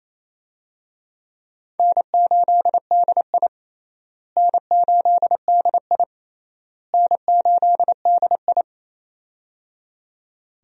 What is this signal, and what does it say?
Morse code, heard as a keyed tone, about 28 wpm, 700 Hz. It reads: N8DI N8DI N8DI
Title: USA Call Signs - 4 Characters - Repeated 3x - 28wpm